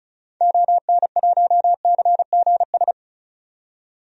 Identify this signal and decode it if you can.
Morse code, heard as a keyed tone, about 36 wpm, 700 Hz. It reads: ON1CGS
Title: DXCC-100 Call Signs - 6 Characters - 35wpm